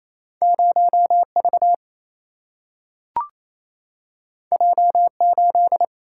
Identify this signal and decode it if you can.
Morse code, heard as a keyed tone, about 28 wpm, 700 Hz. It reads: 0 V J 8
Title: Single-Letter Single-Number 28wpm